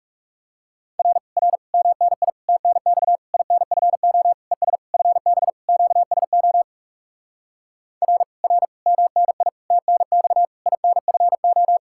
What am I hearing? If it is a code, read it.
Morse code, heard as a keyed tone, about 40 wpm, 700 Hz. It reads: R R MNI TNX INFO ES FB QSO R R MNI TNX INFO